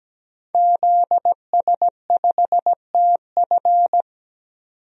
Morse code, heard as a keyed tone, about 17 wpm, 700 Hz. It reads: ZS5TF
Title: DXCC-100 Call Signs - 5 Characters - 17wpm